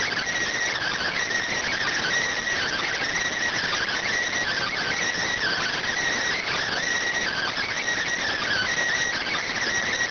Emergency Alert System (SAME) tones (header bursts), heard as harsh buzzing bursts. It is heard from 0.7 to 9.5 s.